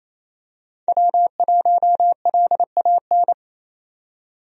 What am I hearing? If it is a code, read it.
Morse code, heard as a keyed tone, about 28 wpm, 700 Hz. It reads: W1LAN
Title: USA Call Signs - 5 Characters - 28wpm